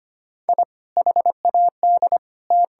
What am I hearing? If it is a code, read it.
Morse code, heard as a keyed tone, about 25 wpm, 700 Hz. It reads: I HAD T
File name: practice / 25wpm / Sentences with 3-Word Phrases - 25wpm